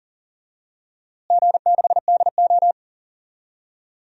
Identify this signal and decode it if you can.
Morse code, heard as a keyed tone, about 40 wpm, 700 Hz. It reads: G6DO
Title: DXCC-100 Call Signs - 4 Characters - Repeated 2x - 40wpm